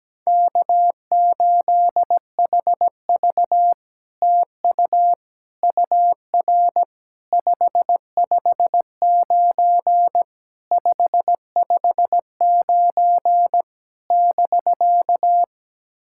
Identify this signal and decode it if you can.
Morse code, heard as a keyed tone, about 18 wpm, 700 Hz. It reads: K8HV TU UR 559 559 <BK>
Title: SOTA QSO Practice 17wpm